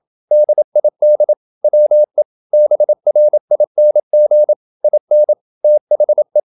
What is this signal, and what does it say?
Morse code, heard as a keyed tone, about 27 wpm, 600 Hz. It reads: DID WE BRING IN THE
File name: Never 0027wpm